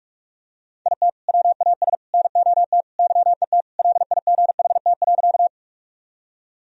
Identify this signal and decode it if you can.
Morse code, heard as a keyed tone, about 46 wpm, 700 Hz. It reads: IT WAS NOT YET LIGHT.
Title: Sentences from Top 1000 Words - Review Part 1 - 45wpm